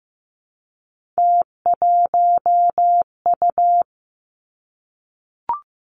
Morse code, heard as a keyed tone, about 16 wpm, 700 Hz. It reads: T1U
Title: Three Challenging Letter-Number Combinations 15wpm